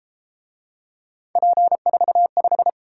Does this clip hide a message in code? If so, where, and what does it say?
Yes: Morse code from 1.4 to 2.7 s, heard as a keyed tone, about 33 wpm, 700 Hz. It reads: P45